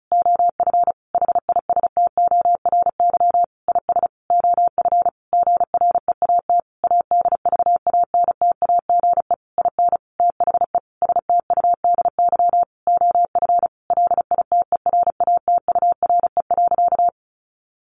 Morse code, heard as a keyed tone, about 34 wpm, 700 Hz. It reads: OF HISTORY IS OF GREAT ADVANTAGE IN THE STUDY OF LITERATURE.
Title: Sentences from Top 900 Words 35wpm